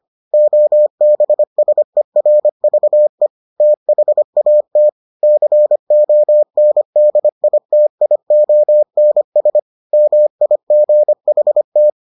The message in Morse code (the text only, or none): OBSERVE THAT CONDITIONS MIGHT